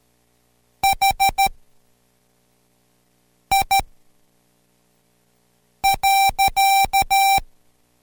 Morse code, heard as a keyed tone, about 13 wpm, spaced as for 4 wpm, 780 Hz. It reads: HI.